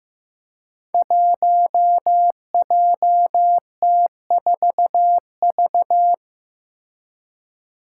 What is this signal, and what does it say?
Morse code, heard as a keyed tone, about 16 wpm, 700 Hz. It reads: 1JT4V
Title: Five Challenging Letter-Number Combinations 15wpm